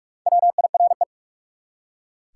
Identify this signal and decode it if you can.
Morse code, heard as a keyed tone, about 44 wpm, 700 Hz. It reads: WIRE
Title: Top 180 QSO Elements 45wpm